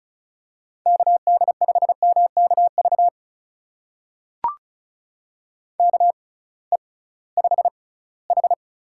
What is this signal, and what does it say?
Morse code, heard as a keyed tone, about 36 wpm, spaced as for 21 wpm, 700 Hz. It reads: KD5MKV KE5H
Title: USA Call Signs - 6 Characters - 6x Character Spacing - 35wpm